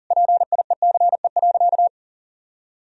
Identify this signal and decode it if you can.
Morse code, heard as a keyed tone, about 37 wpm, 700 Hz. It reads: PIECE.
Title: Sentences from Top 1000 Words - Review Part 3 - 40wpm